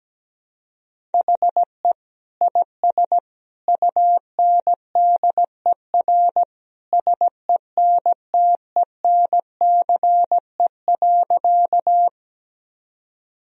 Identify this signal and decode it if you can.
Morse code, heard as a keyed tone, about 17 wpm, 700 Hz. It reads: HE IS UNDER SENTENCE.
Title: Sentences from Top 1000 Words - Review Part 1 - 17wpm